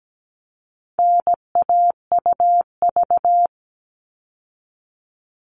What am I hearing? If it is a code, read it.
Morse code, heard as a keyed tone, about 17 wpm, 700 Hz. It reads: NAUV
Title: All Cut Numbers - 4-Digit 17wpm